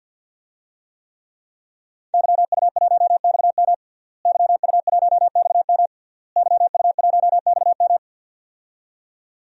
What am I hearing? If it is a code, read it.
Morse code, heard as a keyed tone, about 50 wpm, 700 Hz. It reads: YU1XM YU1XM YU1XM
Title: DXCC-100 Call Signs - 5 Characters - Repeated 3x - 50wpm